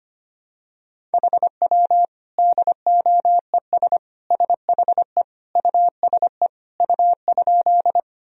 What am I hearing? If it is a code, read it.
Morse code, heard as a keyed tone, about 25 wpm, 700 Hz. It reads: HW DOES SHE USE U?
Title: Sentences from Top 200 Words and Abbreviations 25wpm